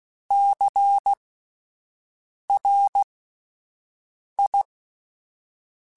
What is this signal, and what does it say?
Morse code, heard as a keyed tone, about 16 wpm, spaced as for 5 wpm, 800 Hz. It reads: CRI
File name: Alphabetical / Alphabetical mp3s / Cs @5